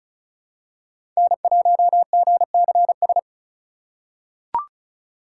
Morse code, heard as a keyed tone, about 36 wpm, 700 Hz. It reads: N1GCS
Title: USA Call Signs - 5 Characters - 3x Character Spacing - 35wpm